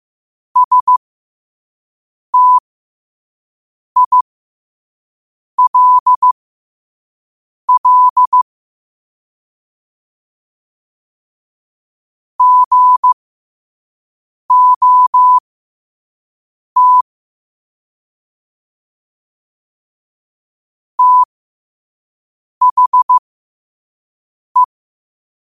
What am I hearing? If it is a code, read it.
Morse code, heard as a keyed tone, about 15 wpm, spaced as for 5 wpm, 1000 Hz. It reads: STILL GOT THE
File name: qotdfiles / 5wpm